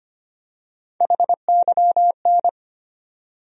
Morse code, heard as a keyed tone, about 25 wpm, 700 Hz. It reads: HYN
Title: USA Call Sign Postfixes - 3 Letters 25wpm